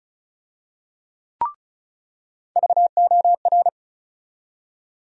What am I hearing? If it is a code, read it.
Morse code, heard as a keyed tone, about 36 wpm, 700 Hz. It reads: VOR
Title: USA Call Sign Postfixes - 3 Letters 35wpm